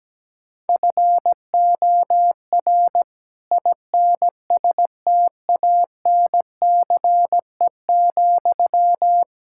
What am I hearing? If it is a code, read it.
Morse code, heard as a keyed tone, about 17 wpm, 700 Hz. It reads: FOR INSTANCE,
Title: Sentences from Top 1000 Words - Review Part 5 - 17wpm